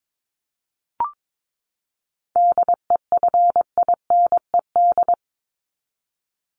Morse code, heard as a keyed tone, about 22 wpm, 700 Hz. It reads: DEFINED
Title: ED Suffix - Top 100 Words with Base Words 22wpm